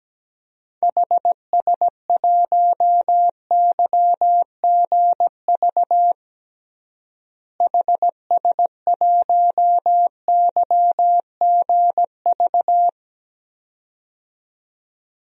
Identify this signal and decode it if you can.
Morse code, heard as a keyed tone, about 17 wpm, 700 Hz. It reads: HS1YGV HS1YGV
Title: DXCC-100 Call Signs - 6 Characters - Repeated 2x - 17wpm